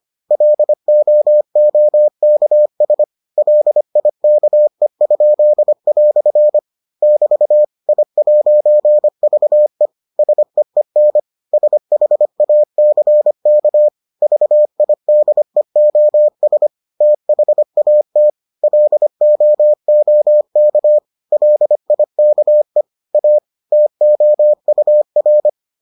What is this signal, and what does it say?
Morse code, heard as a keyed tone, about 25 wpm, 600 Hz. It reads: LOOKS LIKE?" <BT> I'VE SEEN SHACK VIDEOS THAT LOOK LIKE A TOUR